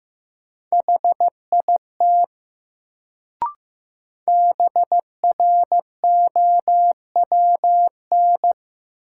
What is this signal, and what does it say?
Morse code, heard as a keyed tone, about 15 wpm, 700 Hz. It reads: HIT BROWN
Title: Sentences from Top 1200 Words - New Word Drill - 15wpm